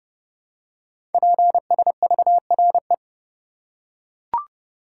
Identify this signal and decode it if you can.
Morse code, heard as a keyed tone, about 30 wpm, 700 Hz. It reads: PSVRE
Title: Code Groups - 5 Letter:Numbers - 30wpm